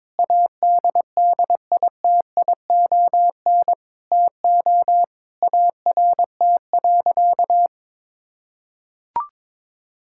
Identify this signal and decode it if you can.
Morse code, heard as a keyed tone, about 22 wpm, 700 Hz. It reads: ADDITION TO ART.
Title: Sentences from Top 1000 Words - Review Part 6 - 22wpm